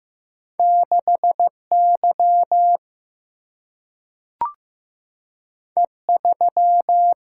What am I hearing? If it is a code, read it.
Morse code, heard as a keyed tone, about 15 wpm, 700 Hz. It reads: 6 Y E 3
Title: Single-Letter Single-Number 15wpm